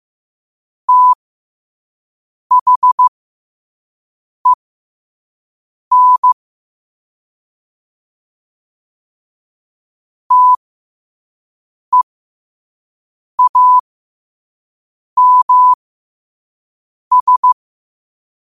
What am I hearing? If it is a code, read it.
Morse code, heard as a keyed tone, about 15 wpm, spaced as for 5 wpm, 1000 Hz. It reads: THEN TEAMS
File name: qotdfiles / 5wpm